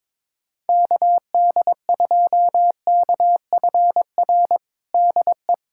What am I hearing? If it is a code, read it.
Morse code, heard as a keyed tone, about 21 wpm, 700 Hz. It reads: KD2KFR DE
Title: USA Callsign DE Callsign 22wpm